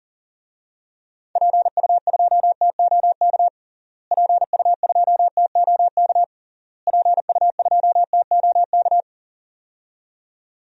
Morse code, heard as a keyed tone, about 40 wpm, 700 Hz. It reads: PU2TOK PU2TOK PU2TOK
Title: DXCC-100 Call Signs - 6 Characters - Repeated 3x - 40wpm